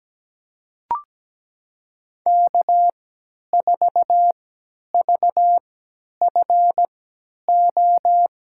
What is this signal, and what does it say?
Morse code, heard as a keyed tone, about 17 wpm, spaced as for 10 wpm, 700 Hz. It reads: K4VFO
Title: USA Call Signs - 3x Character Spacing - 17wpm